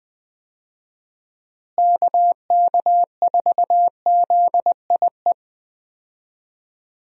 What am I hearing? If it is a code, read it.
Morse code, heard as a keyed tone, about 21 wpm, 700 Hz. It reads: KK4ZIE
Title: USA Call Signs - 6 Characters - Repeated 3x - 20wpm